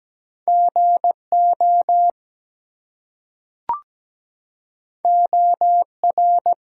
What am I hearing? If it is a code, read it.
Morse code, heard as a keyed tone, about 17 wpm, 700 Hz. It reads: GO OR
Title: Two Letter Words - 17wpm